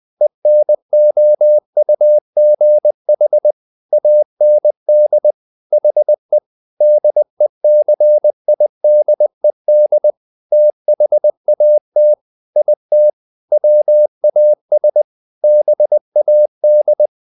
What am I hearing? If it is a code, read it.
Morse code, heard as a keyed tone, about 20 wpm, 600 Hz. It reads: ENOUGH AND HE DECIDED THAT IT WAS BAD